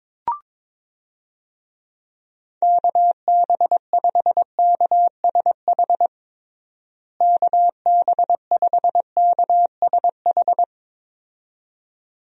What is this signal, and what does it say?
Morse code, heard as a keyed tone, about 22 wpm, 700 Hz. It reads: KB5KSH KB5KSH
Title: USA Call Signs - 6 Characters - Repeated 2x - 22wpm